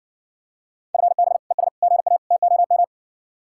Morse code, heard as a keyed tone, about 60 wpm, 700 Hz. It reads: FB ES GA TOM